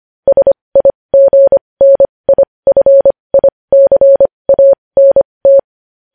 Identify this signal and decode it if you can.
Morse code, heard as a keyed tone, about 25 wpm, spaced as for 20 wpm, 550 Hz. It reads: SIGNIFICANT
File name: podcast / News headlines-1x-25x20